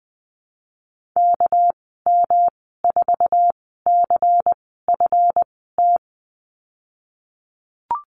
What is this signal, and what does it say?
Morse code, heard as a keyed tone, about 20 wpm, spaced as for 15 wpm, 700 Hz. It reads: KM4CFT
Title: 207 Lesson 33g - Letter j - Call Signs 20wpm